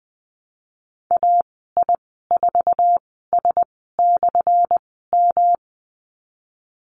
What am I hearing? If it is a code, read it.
Morse code, heard as a keyed tone, about 20 wpm, spaced as for 15 wpm, 700 Hz. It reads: AI4S/M
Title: 179 Lesson 29g - Slash - Call Signs 20wpm